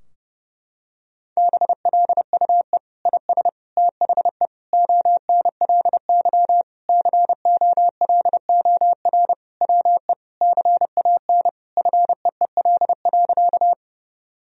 Morse code, heard as a keyed tone, about 30 wpm, 700 Hz. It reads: BLUE IS THE ONLY COLOR WE CAN FEEL.
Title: Sentences from Top 1000 Words - Review Part 3 - 30wpm